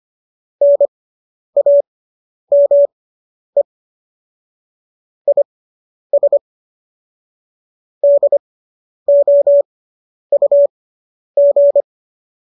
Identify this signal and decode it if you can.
Morse code, heard as a keyed tone, about 25 wpm, spaced as for 10 wpm, 570 Hz. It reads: NAME IS DOUG